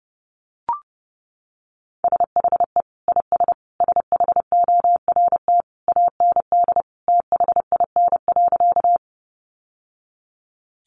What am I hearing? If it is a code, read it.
Morse code, heard as a keyed tone, about 30 wpm, 700 Hz. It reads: SHE IS SHORT AND THIN.
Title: Sentences from Top 1200 Words 30wpm